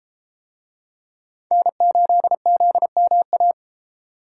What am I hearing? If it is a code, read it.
Morse code, heard as a keyed tone, about 31 wpm, 700 Hz. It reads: N8ZMA